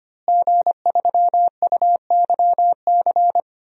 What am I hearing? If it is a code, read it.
Morse code, heard as a keyed tone, about 25 wpm, 700 Hz. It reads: G3UYC
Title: DXCC-100 Call Signs - 5 Characters - Repeated 3x - 25wpm